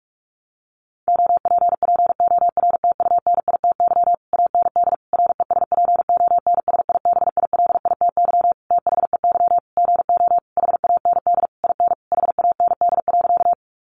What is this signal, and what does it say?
Morse code, heard as a keyed tone, about 44 wpm, 700 Hz. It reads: OPPORTUNITY AND RESPONSIBILITY THEY GO HAND IN HAND.
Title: Sentences from Top 1200 Words 45wpm